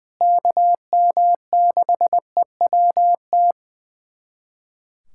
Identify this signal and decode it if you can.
Morse code, heard as a keyed tone, about 20 wpm, 700 Hz. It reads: KM6EWT
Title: USA Callsign DE Callsign - 6 Characters - 20wpm